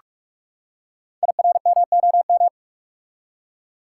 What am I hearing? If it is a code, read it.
Morse code, heard as a keyed tone, about 45 wpm, 700 Hz. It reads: IRMOM
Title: Code Groups - 5 Letters - 45wpm